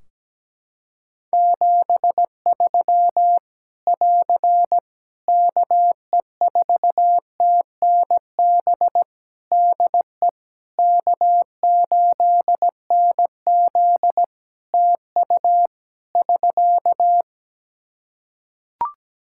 Morse code, heard as a keyed tone, about 17 wpm, 700 Hz. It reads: 73 <AR> KE4TNB DE K8NZ TU <SK>
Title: Practice QSO - Exchange 4 - Ending - 17wpm